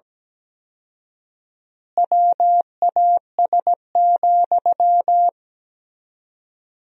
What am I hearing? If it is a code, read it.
Morse code, heard as a keyed tone, about 17 wpm, 700 Hz. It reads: WAS,
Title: 4x Word-Spacing - Sentences from Top 400 Words - 17wpm